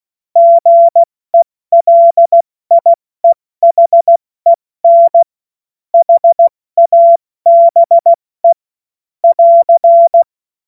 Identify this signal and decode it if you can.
Morse code, heard as a keyed tone, about 16 wpm, 680 Hz. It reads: GELIEHEN HABE <AR>
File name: Morsekurs / Witze / Witz 10. Januar 2023 70 F80